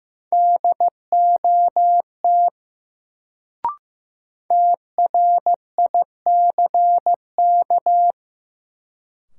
Morse code, heard as a keyed tone, about 15 wpm, 700 Hz. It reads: DOT TRICK
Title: Sentences from Top 1500 Words - New Word Drill Alt - 15wpm